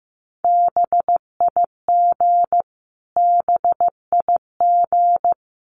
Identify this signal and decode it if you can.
Morse code, heard as a keyed tone, about 14 wpm, 700 Hz. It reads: BIG BIG
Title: Top 200 Words Repeated 15wpm